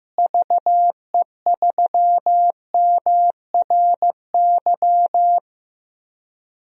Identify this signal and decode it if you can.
Morse code, heard as a keyed tone, about 15 wpm, 700 Hz. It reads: VE3MRY
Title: DXCC-100 Call Signs - 6 Characters - Repeated 3x - 15wpm